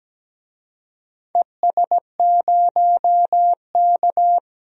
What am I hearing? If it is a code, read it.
Morse code, heard as a keyed tone, about 17 wpm, 700 Hz. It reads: ES0K
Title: DXCC-100 Call Signs - 4 Characters - 17wpm